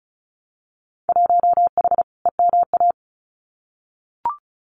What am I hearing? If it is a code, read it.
Morse code, heard as a keyed tone, about 36 wpm, 700 Hz. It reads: 1H EMA
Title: Winter Field Day Exchanges Encore 35wpm